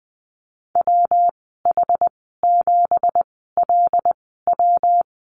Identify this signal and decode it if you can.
Morse code, heard as a keyed tone, about 20 wpm, spaced as for 14 wpm, 700 Hz. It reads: WH7LW
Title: 167 Lesson 27g - Number 7 - Call Signs 20wpm